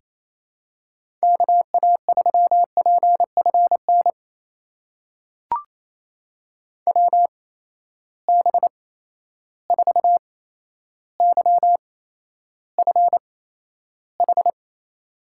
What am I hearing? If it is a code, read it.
Morse code, heard as a keyed tone, about 28 wpm, spaced as for 12 wpm, 700 Hz. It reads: KA3PFN WB4YFH
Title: USA Call Signs - 6 Characters - 8x Character Spacing - 28wpm